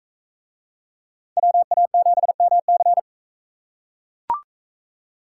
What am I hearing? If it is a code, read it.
Morse code, heard as a keyed tone, about 43 wpm, 700 Hz. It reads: WA7MC